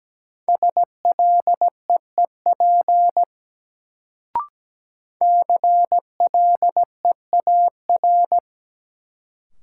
Morse code, heard as a keyed tone, about 17 wpm, 700 Hz. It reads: SLEEP CLEAR
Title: Sentences from Top 500 Words - New Word Drill - 17wpm